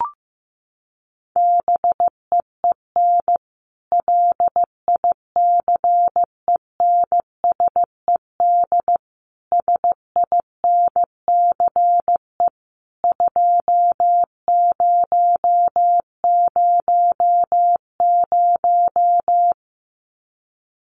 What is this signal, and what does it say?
Morse code, heard as a keyed tone, about 15 wpm, 700 Hz. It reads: BEEN LICENSED SINCE 2000
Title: Common QSO Phrases 15wpm